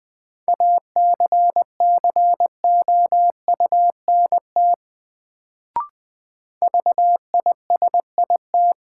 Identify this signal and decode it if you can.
Morse code, heard as a keyed tone, about 20 wpm, 700 Hz. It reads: ACCOUNT VISIT
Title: Sentences from Top 900 Words - New Word Drill - 20wpm